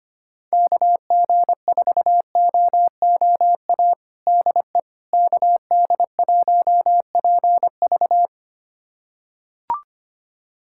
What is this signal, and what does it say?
Morse code, heard as a keyed tone, about 25 wpm, 700 Hz. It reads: KG4OOA DE KD1PV
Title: USA Callsign DE Callsign 25wpm